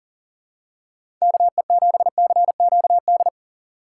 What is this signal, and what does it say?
Morse code, heard as a keyed tone, about 40 wpm, 700 Hz. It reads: KE7CQD
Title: USA Call Signs - 6x Character Spacing - 40wpm